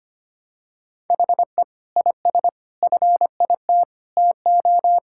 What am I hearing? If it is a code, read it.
Morse code, heard as a keyed tone, about 25 wpm, 700 Hz. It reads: HE IS FIT TO
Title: Sentences from Top 1000 Words - Review Part 5 - 25wpm